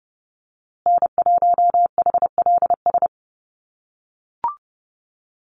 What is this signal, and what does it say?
Morse code, heard as a keyed tone, about 30 wpm, 700 Hz. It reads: N1HLS
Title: USA Call Signs Repeated 3x 30wpm